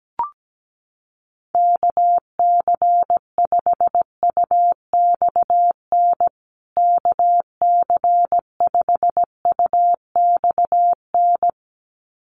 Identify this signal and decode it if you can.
Morse code, heard as a keyed tone, about 17 wpm, 700 Hz. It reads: KC5UXN KC5UXN
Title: USA Call Signs Repeated 2x 17wpm